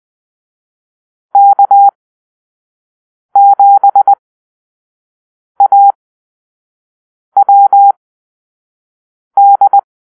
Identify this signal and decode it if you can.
Morse code, heard as a keyed tone, about 20 wpm, spaced as for 5 wpm, 800 Hz. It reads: K7AWD
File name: arrl / Call-Sign Drill-1